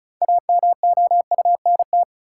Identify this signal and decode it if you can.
Morse code, heard as a keyed tone, about 36 wpm, 700 Hz. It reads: AMOUNT
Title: Sentences from Top 700 Words - New Word Drill - 35wpm